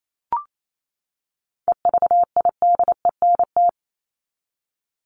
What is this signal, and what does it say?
Morse code, heard as a keyed tone, about 28 wpm, 700 Hz. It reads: EVIDENT
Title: LY Suffix - Top 100 Words with Base Words 28wpm